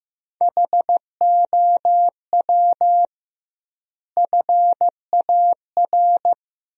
Morse code, heard as a keyed tone, about 16 wpm, 700 Hz. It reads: HOW FAR
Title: 2x Word-Spacing - Sentences from Top 400 Words - 15wpm